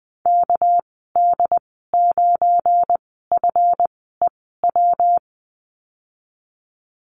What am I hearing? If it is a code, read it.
Morse code, heard as a keyed tone, about 19 wpm, 700 Hz. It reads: K D 9 F E W
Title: 238 Lesson 38g - Letter q - Call Signs 20wpm